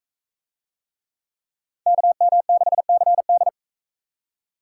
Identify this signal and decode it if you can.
Morse code, heard as a keyed tone, about 38 wpm, 700 Hz. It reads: KM6CD